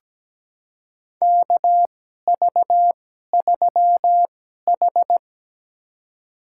Morse code, heard as a keyed tone, about 17 wpm, spaced as for 12 wpm, 700 Hz. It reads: KV3H
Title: USA Call Signs - 4 Characters - 2x Character Spacing - 17wpm